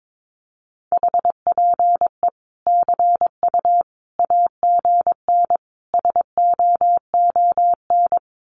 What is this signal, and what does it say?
Morse code, heard as a keyed tone, about 22 wpm, 700 Hz. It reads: HPE CU AGN SOON
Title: Common QSO Phrases 22wpm